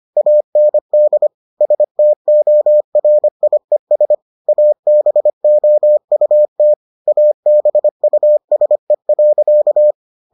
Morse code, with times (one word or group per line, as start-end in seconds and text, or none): 0.2-1.3 AND
1.6-4.1 STORIES
4.5-6.7 ABOUT
7.1-9.9 ABUSE.